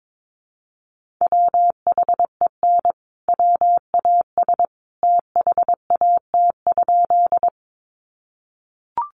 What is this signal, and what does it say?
Morse code, heard as a keyed tone, about 22 wpm, 700 Hz. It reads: WHEN WAS THAT?
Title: Sentences from Top 100 Words 22wpm